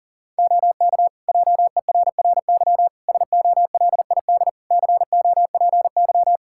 Morse code, heard as a keyed tone, about 40 wpm, 700 Hz. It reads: OK JERRY SOLID COPY
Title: Practice QSO - Exchange 3 - More Info - 40wpm